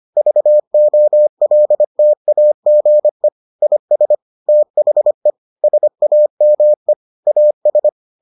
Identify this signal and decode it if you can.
Morse code, heard as a keyed tone, about 25 wpm, 600 Hz. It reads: VOLTAGE IS THE SAME AS